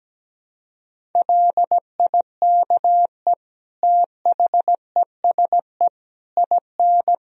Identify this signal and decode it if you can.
Morse code, heard as a keyed tone, about 17 wpm, 700 Hz. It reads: LIKE THESE IN
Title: Sentences from Top 500 Words and Abbreviations 17wpm